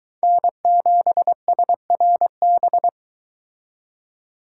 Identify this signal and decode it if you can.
Morse code, heard as a keyed tone, about 22 wpm, 700 Hz. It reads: N7SRB